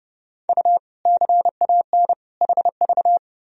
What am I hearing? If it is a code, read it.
Morse code, heard as a keyed tone, about 32 wpm, 700 Hz. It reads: U CAN HV
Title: Sentences from Top 100 Words and Abbreviations 30wpm